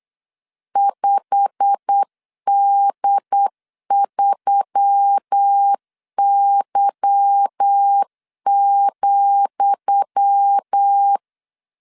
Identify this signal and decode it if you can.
Morse code, heard as a keyed tone, about 8 wpm, 800 Hz. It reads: 5D3Y,
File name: LEZIONI / 10 SIGNS / 10D COMA 8wpm 3749